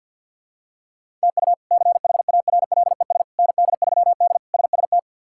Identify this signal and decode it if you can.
Morse code, heard as a keyed tone, about 48 wpm, 700 Hz. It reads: TU CHARLES ND3D SST
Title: K1USN SST QSO Practice - 50wpm